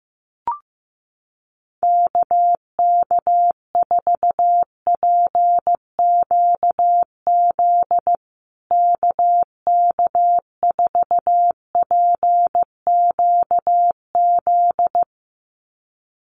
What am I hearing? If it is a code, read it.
Morse code, heard as a keyed tone, about 15 wpm, 700 Hz. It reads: KK4PQZ KK4PQZ
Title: USA Call Signs Repeated 2x 15wpm